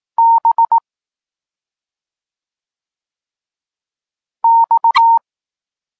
Morse code, heard as a keyed tone, about 18 wpm, 920 Hz. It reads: B X